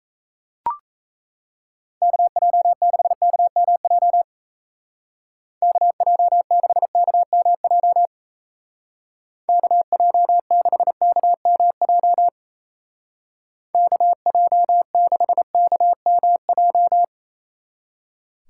Morse code, heard as a keyed tone, about 35 wpm, 700 Hz. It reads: KJ6KMJ KJ6KMJ KJ6KMJ KJ6KMJ
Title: Speed Racing - USA 2x3 Call Signs 28wpm